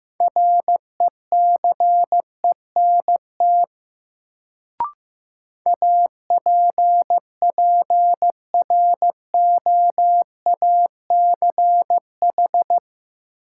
Morse code, heard as a keyed tone, about 15 wpm, 700 Hz. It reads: RECENT APPROACH
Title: Sentences from Top 1400 Words - New Word Drill Alt - 15wpm